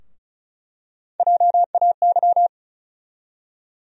Morse code, heard as a keyed tone, about 36 wpm, 700 Hz. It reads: JAY
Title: USA Call Sign Postfixes - 3 Letters 35wpm